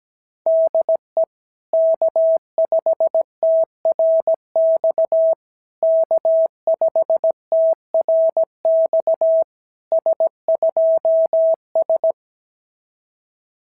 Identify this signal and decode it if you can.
Morse code, heard as a keyed tone, about 16 wpm, 650 Hz. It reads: DE K5TRX K5TRX S2S
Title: SOTA QSO Practice 17wpm